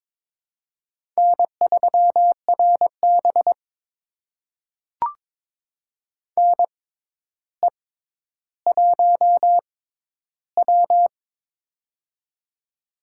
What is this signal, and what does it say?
Morse code, heard as a keyed tone, about 22 wpm, spaced as for 11 wpm, 700 Hz. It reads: N3RB NE1W
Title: USA Call Signs - 4 Characters - 6x Character Spacing - 22wpm